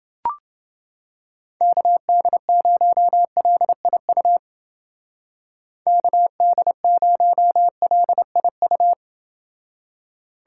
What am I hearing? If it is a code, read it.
Morse code, heard as a keyed tone, about 30 wpm, 700 Hz. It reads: KD0LIU KD0LIU